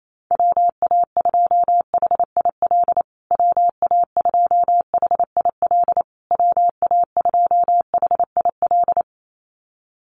Morse code, heard as a keyed tone, about 28 wpm, 700 Hz. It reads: WA2HIL WA2HIL WA2HIL
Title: USA Call Signs Repeated 3x 28wpm